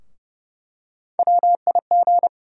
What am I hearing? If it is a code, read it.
Morse code, heard as a keyed tone, about 30 wpm, 700 Hz. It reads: WIG
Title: USA Call Sign Postfixes - 3 Letters 30wpm